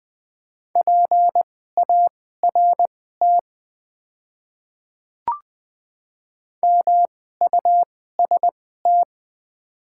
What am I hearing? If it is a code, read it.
Morse code, heard as a keyed tone, about 20 wpm, spaced as for 14 wpm, 700 Hz. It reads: PART MUST
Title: 2x Character Spacing - Top 200 Words - 20wpm